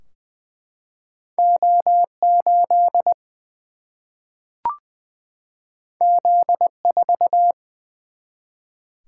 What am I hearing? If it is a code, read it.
Morse code, heard as a keyed tone, about 20 wpm, 700 Hz. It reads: O 8 Z 4
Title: Single-Letter Single-Number 20wpm